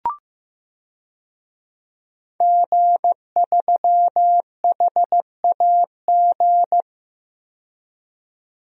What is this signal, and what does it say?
Morse code, heard as a keyed tone, about 15 wpm, 700 Hz. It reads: G3HAG
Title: DXCC-100 Call Signs - 5 Characters - 15wpm